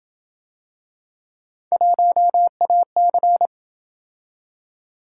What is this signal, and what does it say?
Morse code, heard as a keyed tone, about 27 wpm, 700 Hz. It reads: 1AC